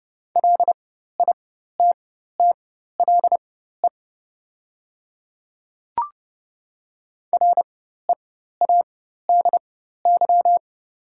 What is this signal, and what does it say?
Morse code, heard as a keyed tone, about 30 wpm, spaced as for 14 wpm, 700 Hz. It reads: LITTLE READY
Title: 4x Character Spacing - Top 500 Words - 30wpm